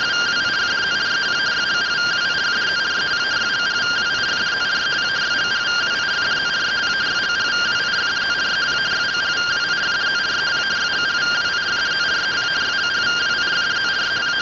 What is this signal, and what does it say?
Two-tone teleprinter signal, about 170 Hz shift.